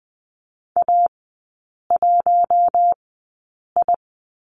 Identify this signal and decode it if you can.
Morse code, heard as a keyed tone, about 20 wpm, spaced as for 8 wpm, 700 Hz. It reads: A1I
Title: 074 Lesson 13e - Letter L - Three Characters 20wpm